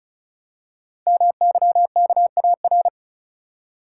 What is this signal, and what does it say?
Morse code, heard as a keyed tone, about 36 wpm, 700 Hz. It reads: MYKAR
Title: Code Groups - 5 Letters - 35wpm